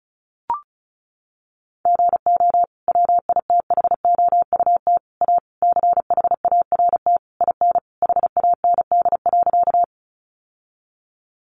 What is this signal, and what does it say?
Morse code, heard as a keyed tone, about 34 wpm, 700 Hz. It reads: GO WITHOUT A CHART IN HAND.
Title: Sentences from Top 1300 Words 35wpm